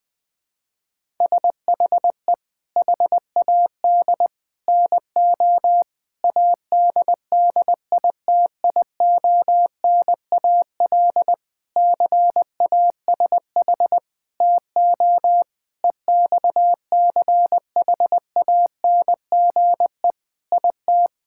Morse code, heard as a keyed tone, about 20 wpm, 700 Hz. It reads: SHE HAD NO ADDITIONAL CASH TO EXCHANGE IT
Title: Sentences from Top 1000 Words - Review Part 6 - 20wpm